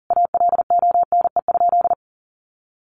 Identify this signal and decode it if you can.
Morse code, heard as a keyed tone, about 40 wpm, 700 Hz. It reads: ALONE?
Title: Sentences from Top 1000 Words 40wpm